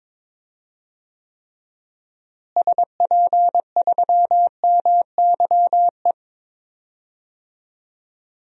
Morse code, heard as a keyed tone, about 22 wpm, 700 Hz. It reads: SP3MYE
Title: DXCC-100 Call Signs - 6 Characters - Repeated 3x - 22wpm